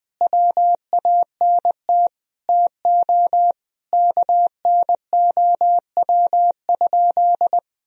Morse code, heard as a keyed tone, about 19 wpm, 700 Hz. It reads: WANT TO KNOW?
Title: Sentences from Top 200 Words and Abbreviations 20wpm